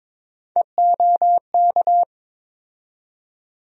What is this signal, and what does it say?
Morse code, heard as a keyed tone, about 22 wpm, 700 Hz. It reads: EOK
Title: USA Call Sign Postfixes - 3 Letters 22wpm